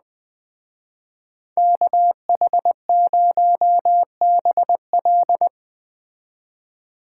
Morse code, heard as a keyed tone, about 20 wpm, 700 Hz. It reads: KH0BL
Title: Speed Racing - USA 2x2 Call Signs 17wpm